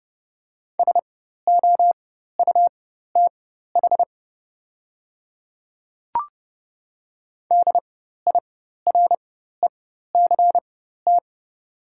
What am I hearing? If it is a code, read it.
Morse code, heard as a keyed tone, about 30 wpm, spaced as for 14 wpm, 690 Hz. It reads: SOUTH DIRECT
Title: 4x Character Spacing - Top 500 Words - 30wpm